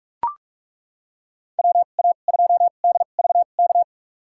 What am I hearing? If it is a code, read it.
Morse code, heard as a keyed tone, about 42 wpm, spaced as for 31 wpm, 700 Hz. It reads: WA2DVK
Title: USA Call Signs - 6 Characters - 2x Character Spacing - 45wpm